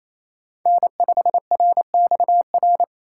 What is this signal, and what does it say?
Morse code, heard as a keyed tone, about 28 wpm, 700 Hz. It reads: N5RXR
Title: USA Call Signs - 5 Characters - 4x Character Spacing - 28wpm